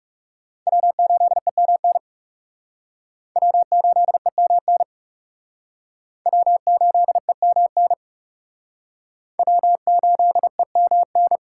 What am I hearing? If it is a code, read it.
Morse code, heard as a keyed tone, about 38 wpm, 700 Hz. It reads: W8EMN W8EMN W8EMN W8EMN